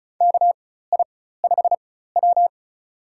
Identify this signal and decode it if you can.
Morse code, heard as a keyed tone, about 36 wpm, spaced as for 16 wpm, 700 Hz. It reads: KI5W
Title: USA Call Signs - 4 Characters - 4x Character Spacing - 35wpm